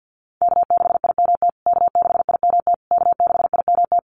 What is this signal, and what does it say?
Morse code, heard as a keyed tone, about 50 wpm, 700 Hz. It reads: K6IMT K6IMT K6IMT
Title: USA Call Signs Repeated 3x 50wpm